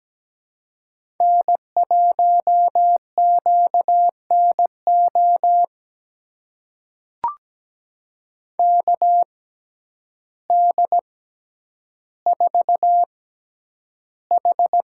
Morse code, heard as a keyed tone, about 17 wpm, spaced as for 9 wpm, 700 Hz. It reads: N1QNO KD4H
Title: USA Call Signs - 5 Characters - 6x Character Spacing - 17wpm